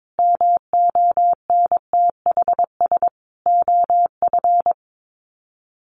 Morse code, heard as a keyed tone, about 22 wpm, 700 Hz. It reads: MONTHS OF
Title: Sets of 5 Words Encore2 22wpm